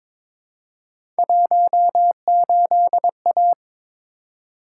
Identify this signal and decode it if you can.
Morse code, heard as a keyed tone, about 22 wpm, 700 Hz. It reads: 18A
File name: practice / 22wpm / Three Letter-Number Combinations - 22wpm